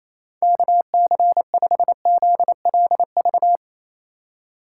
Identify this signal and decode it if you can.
Morse code, heard as a keyed tone, about 28 wpm, 700 Hz. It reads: KC5ZLV
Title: USA Call Signs - 6 Characters - 3x Character Spacing - 28wpm